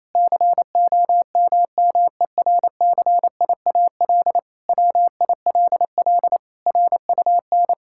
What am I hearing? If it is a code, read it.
Morse code, heard as a keyed tone, about 28 wpm, 700 Hz. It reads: COMMERCIAL WILL RUN
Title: Sentences from Top 1000 Words - Review Part 5 - 28wpm